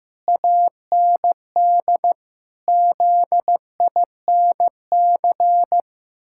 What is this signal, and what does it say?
Morse code, heard as a keyed tone, about 15 wpm, 700 Hz. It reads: AND ZINC
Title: Pangram Challenge v2 - 15wpm